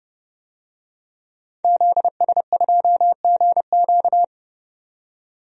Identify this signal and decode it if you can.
Morse code, heard as a keyed tone, about 30 wpm, 700 Hz. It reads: ZS2GQ
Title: DXCC-100 Call Signs - 5 Characters - 30wpm